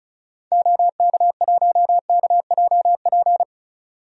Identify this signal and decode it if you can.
Morse code, heard as a keyed tone, about 34 wpm, 700 Hz. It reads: OK1KJP
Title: DXCC-100 Call Signs - 6 Characters - 35wpm